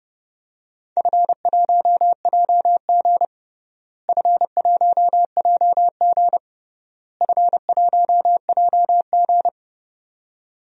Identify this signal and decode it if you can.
Morse code, heard as a keyed tone, about 32 wpm, 700 Hz. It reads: F1JG F1JG F1JG
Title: DXCC-100 Call Signs - 4 Characters - Repeated 3x - 30wpm